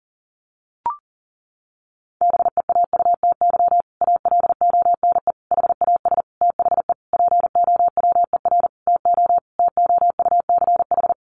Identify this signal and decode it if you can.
Morse code, heard as a keyed tone, about 40 wpm, 700 Hz. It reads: BEAUTY ALONE HAS THE POWER TO TOUCH
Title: Sentences from Top 1200 Words 40wpm